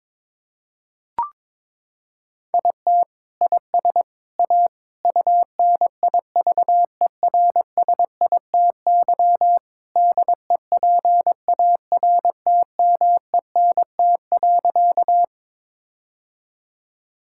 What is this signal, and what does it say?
Morse code, heard as a keyed tone, about 23 wpm, 700 Hz. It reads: IT IS A UNIVERSITY DEPARTMENT.
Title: Sentences from Top 1000 Words - Review Part 5 - 22wpm